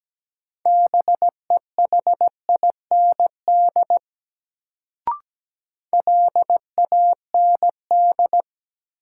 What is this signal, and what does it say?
Morse code, heard as a keyed tone, about 17 wpm, 700 Hz. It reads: BEHIND LAND
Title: Sentences from Top 500 Words - New Word Drill - 17wpm